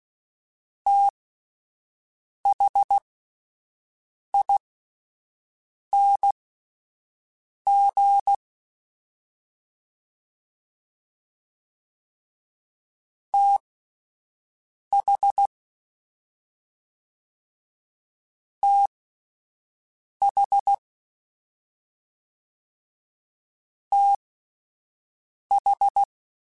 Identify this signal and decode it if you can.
Morse code, heard as a keyed tone, about 16 wpm, spaced as for 5 wpm, 800 Hz. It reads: THING TH TH TH
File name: Combinations / Combinations-mp3s / Combinations@5